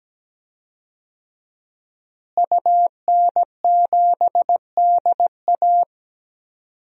Morse code, heard as a keyed tone, about 17 wpm, 700 Hz. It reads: UN7DA
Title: DXCC-100 Call Signs - 5 Characters - Repeated 2x - 17wpm